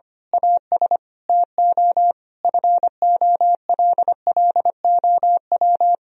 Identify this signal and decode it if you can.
Morse code, heard as a keyed tone, about 25 wpm, 700 Hz. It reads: AS TO FOLLOW
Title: Sentences from Top 1000 Words - Review Part 1 - 25wpm